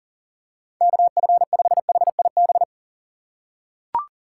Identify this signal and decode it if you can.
Morse code, heard as a keyed tone, about 40 wpm, 700 Hz. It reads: KF5HIB
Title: USA Call Signs - 6 Characters - 4x Character Spacing - 40wpm